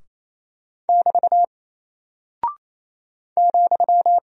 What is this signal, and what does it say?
Morse code, heard as a keyed tone, about 28 wpm, 700 Hz. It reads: <BT> ,
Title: Single Character (Letter:Number:Punctuation:Prosign) Alt 28wpm